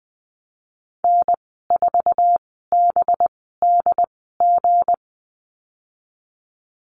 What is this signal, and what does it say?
Morse code, heard as a keyed tone, about 20 wpm, spaced as for 15 wpm, 700 Hz. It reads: N4BDG
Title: 231 Lesson 37g - Letter x - Call Signs 20wpm